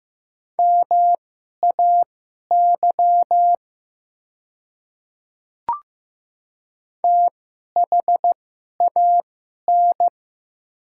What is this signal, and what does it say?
Morse code, heard as a keyed tone, about 15 wpm, spaced as for 11 wpm, 690 Hz. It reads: MAY THAN
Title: 2x Character Spacing - Top 100 Words - 15wpm